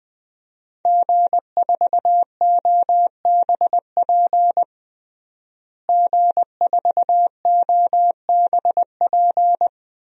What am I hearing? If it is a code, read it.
Morse code, heard as a keyed tone, about 20 wpm, 700 Hz. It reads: G4OBP G4OBP
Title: DXCC-100 Call Signs - 5 Characters - Repeated 2x - 20wpm